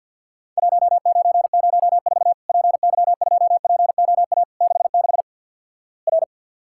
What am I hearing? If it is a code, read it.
Morse code, heard as a keyed tone, about 48 wpm, 700 Hz. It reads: 1904 PY2POA 66 R
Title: Worked All Europe DX Contest - QTC Exchange with Repeats - 50wpm